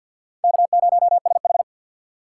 Morse code, heard as a keyed tone, about 46 wpm, 690 Hz. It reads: K0SH
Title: USA Call Signs - 3x Character Spacing - 50wpm